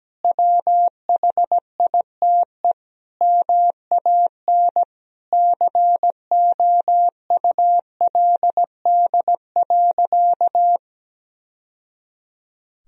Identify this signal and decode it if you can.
Morse code, heard as a keyed tone, about 17 wpm, 700 Hz. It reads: WHITE MAN COULD.
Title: Sentences from Top 1000 Words - Review Part 1 - 17wpm